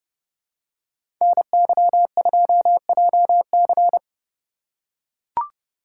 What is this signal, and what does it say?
Morse code, heard as a keyed tone, about 29 wpm, 700 Hz. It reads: NY2JC